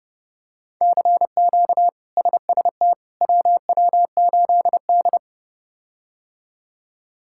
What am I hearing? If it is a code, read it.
Morse code, heard as a keyed tone, about 31 wpm, 700 Hz. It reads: CQ SST WW8D
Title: K1USN SST QSO Practice - 30wpm